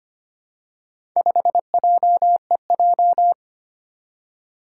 Morse code, heard as a keyed tone, about 25 wpm, 700 Hz. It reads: 5JEJ
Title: Four Challenging Letter-Number Combinations 25wpm